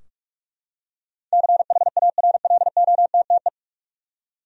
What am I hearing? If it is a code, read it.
Morse code, heard as a keyed tone, about 46 wpm, 700 Hz. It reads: CHARLOTTE
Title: USA Common Names - 45wpm